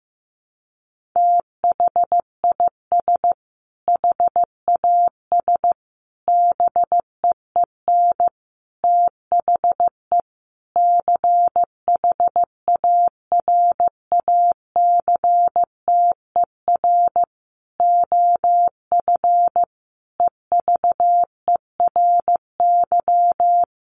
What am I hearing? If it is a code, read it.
Morse code, heard as a keyed tone, about 15 wpm, 700 Hz. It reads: THIS HAS BEEN THE CHARACTER OF EVERY
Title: Sentences from Top 1400 Words 15wpm